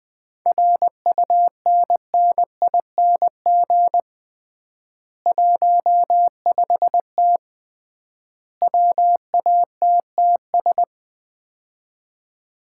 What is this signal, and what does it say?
Morse code, heard as a keyed tone, about 20 wpm, 700 Hz. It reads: RUNNING 15T WATTS
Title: Common QSO Phrases - 3x Word-Spacing - 20wpm